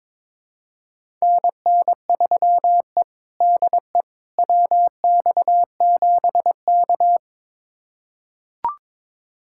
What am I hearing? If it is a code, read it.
Morse code, heard as a keyed tone, about 22 wpm, 700 Hz. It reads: NN3E DE WX7K
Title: USA Callsign DE Callsign - 4 Characters - 22wpm